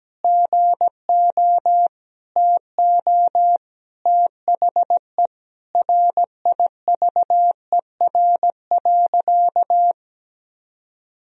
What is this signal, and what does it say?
Morse code, heard as a keyed tone, about 17 wpm, 700 Hz. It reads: GO TO THE RIVER.
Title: Sentences from Top 1000 Words - Review Part 4 - 17wpm